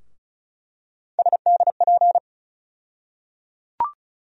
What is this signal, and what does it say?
Morse code, heard as a keyed tone, about 34 wpm, 700 Hz. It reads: SDP
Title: USA Call Sign Postfixes - 3 Letters 35wpm